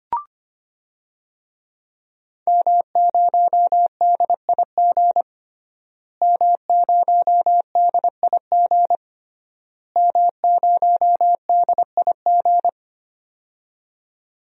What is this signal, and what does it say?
Morse code, heard as a keyed tone, about 25 wpm, 700 Hz. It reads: M0DIG M0DIG M0DIG
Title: DXCC-100 Call Signs - 5 Characters - Repeated 3x - 25wpm